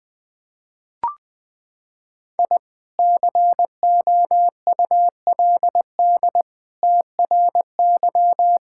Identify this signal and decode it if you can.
Morse code, heard as a keyed tone, about 20 wpm, 700 Hz. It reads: I COULD TRY
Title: Sentences with 2-Word Phrases - 20wpm